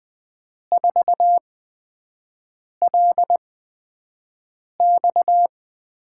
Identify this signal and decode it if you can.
Morse code, heard as a keyed tone, about 20 wpm, 700 Hz. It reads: 4 L X
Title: USA Call Signs - 5 Characters - 8x Character Spacing - 20wpm